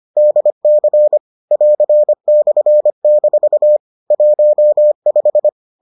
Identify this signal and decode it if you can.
Morse code, heard as a keyed tone, about 25 wpm, 600 Hz. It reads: DC <AR>/- 15